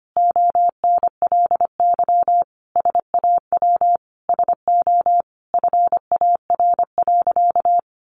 Morse code, heard as a keyed tone, about 25 wpm, 700 Hz. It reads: ONLY SAW SO FAR.
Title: Sentences from Top 800 Words 25wpm